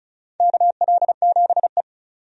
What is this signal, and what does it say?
Morse code, heard as a keyed tone, about 36 wpm, 690 Hz. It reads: KL7E
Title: DXCC-100 Call Signs - 4 Characters - 35wpm